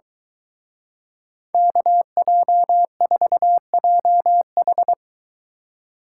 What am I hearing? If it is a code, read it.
Morse code, heard as a keyed tone, about 23 wpm, 700 Hz. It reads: KJ4JH